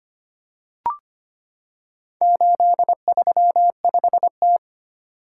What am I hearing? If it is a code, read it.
Morse code, heard as a keyed tone, about 25 wpm, 700 Hz. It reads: 835T
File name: practice / 25wpm / Four Letter-Number Combinations - 25wpm